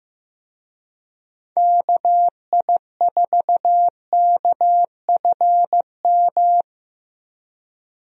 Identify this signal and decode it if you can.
Morse code, heard as a keyed tone, about 15 wpm, 700 Hz. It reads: KI4KFM
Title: USA Call Signs - 6 Characters - Repeated 3x - 15wpm